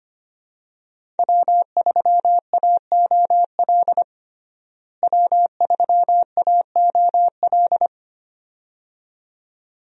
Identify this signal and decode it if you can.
Morse code, heard as a keyed tone, about 25 wpm, 700 Hz. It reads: W3AOL W3AOL
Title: USA Call Signs - 5 Characters - Repeated 2x - 25wpm